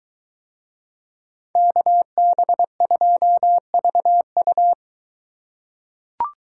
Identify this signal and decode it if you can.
Morse code, heard as a keyed tone, about 23 wpm, 700 Hz. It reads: KB2VU